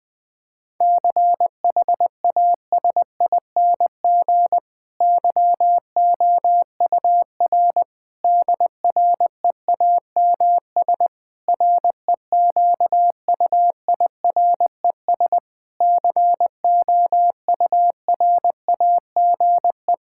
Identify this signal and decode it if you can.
Morse code, heard as a keyed tone, about 20 wpm, 700 Hz. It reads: CHASING YOUR DREAMS REQUIRES COURAGE
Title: Difficult Sentences with Hints - 20wpm